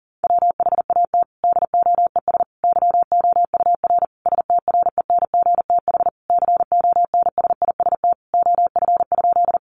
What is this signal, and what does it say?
Morse code, heard as a keyed tone, about 40 wpm, 700 Hz. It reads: WHAT DOES YOUR STRENGTH CONSIST OF?
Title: Sentences from Top 1400 Words 40wpm